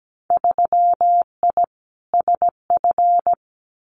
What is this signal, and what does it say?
Morse code, heard as a keyed tone, about 17 wpm, 700 Hz. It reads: 3I SF
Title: Winter Field Day Exchanges 17wpm